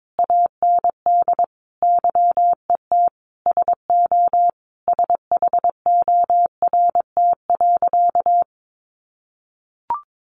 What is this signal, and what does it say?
Morse code, heard as a keyed tone, about 22 wpm, 700 Hz. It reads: AND YET SO SHORT.
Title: Sentences from Top 500 Words 22wpm